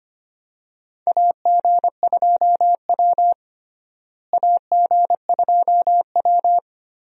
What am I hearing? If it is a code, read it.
Morse code, heard as a keyed tone, about 25 wpm, 700 Hz. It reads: AG2W AG2W
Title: USA Call Signs - 4 Characters - Repeated 3x - 25wpm